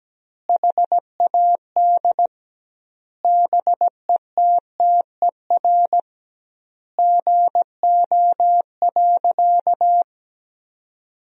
Morse code, heard as a keyed tone, about 17 wpm, 700 Hz. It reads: HAD BETTER GO.
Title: 2x Word-Spacing - Sentences from Top 400 Words - 17wpm